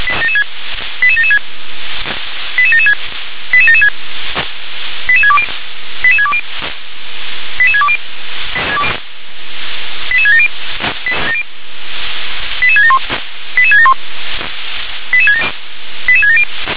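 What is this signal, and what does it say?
Five-tone selective call (ZVEI), heard as quick runs of five tones, tones of about 70 ms. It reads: 88776 88776 88776 88620 88620 88620 88670 88671 88671 88670
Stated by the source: ZVEI five-tone call received on SDR.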